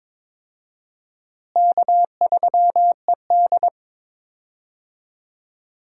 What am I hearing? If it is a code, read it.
Morse code, heard as a keyed tone, about 22 wpm, 700 Hz. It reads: K3ED
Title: USA Call Signs - 4 Characters - Repeated 3x - 22wpm